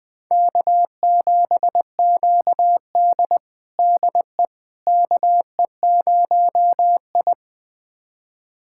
Morse code, heard as a keyed tone, about 21 wpm, 700 Hz. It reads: K7QD DE KE0I
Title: USA Callsign DE Callsign - 4 Characters - 20wpm